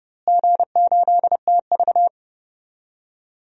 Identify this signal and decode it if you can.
Morse code, heard as a keyed tone, about 32 wpm, 700 Hz. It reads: G8TV
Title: DXCC-100 Call Signs - 4 Characters - Repeated 2x - 30wpm